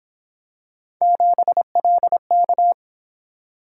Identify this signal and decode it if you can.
Morse code, heard as a keyed tone, about 27 wpm, 700 Hz. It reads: 7LK
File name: Speed-Racing - Three Letter-Number Combinations - 22wpm